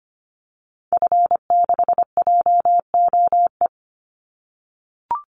Morse code, heard as a keyed tone, about 25 wpm, 700 Hz. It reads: F6JOE
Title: Call Signs - 25wpm